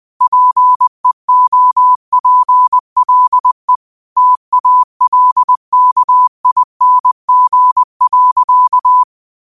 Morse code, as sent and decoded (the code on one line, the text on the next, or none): .--. --- .--. .-.. / - .- .-.. -.- .. -. --. .-.-.-
PEOPLE TALKING.